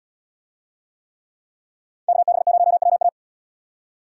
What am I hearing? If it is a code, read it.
Morse code, heard as a keyed tone, about 67 wpm, 700 Hz. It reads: KB0WA